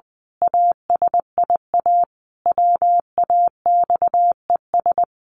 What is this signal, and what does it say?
Morse code, heard as a keyed tone, about 20 wpm, 700 Hz. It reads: ASIA WAXES
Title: A Sound Of Thunder 20wpm